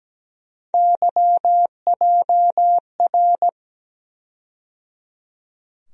Morse code, heard as a keyed tone, about 17 wpm, 700 Hz. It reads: YJR
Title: USA Call Sign Postfixes - 3 Letters 17wpm